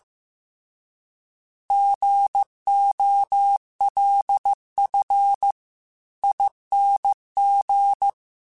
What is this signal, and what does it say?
Morse code, heard as a keyed tone, about 15 wpm, 800 Hz. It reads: GOLFING